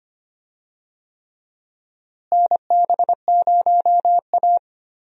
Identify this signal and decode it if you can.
Morse code, heard as a keyed tone, about 25 wpm, 700 Hz. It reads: NB0A
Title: USA Call Signs - 4 Characters - Repeated 3x - 25wpm